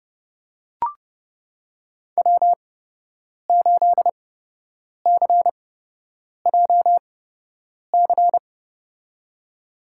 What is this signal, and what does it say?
Morse code, heard as a keyed tone, about 30 wpm, spaced as for 8 wpm, 700 Hz. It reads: W8CJC
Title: USA Call Signs - 8x Character Spacing - 30wpm